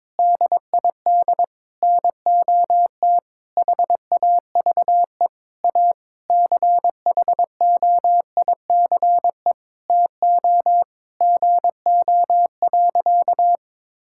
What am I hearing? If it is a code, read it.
Morse code, heard as a keyed tone, about 22 wpm, 700 Hz. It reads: DID NOT HAVE A CHOICE TO GO.
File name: practice / 22wpm / Sentences with 4-Word Phrases - 22wpm